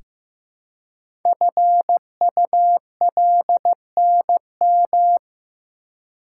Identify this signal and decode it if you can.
Morse code, heard as a keyed tone, about 15 wpm, 700 Hz. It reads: FULNM
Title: Code Groups - 5 Letters - 15wpm